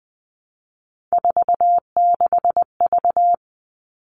Morse code, heard as a keyed tone, about 20 wpm, 700 Hz. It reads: 46V
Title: All Cut Numbers - 3-Digit 20wpm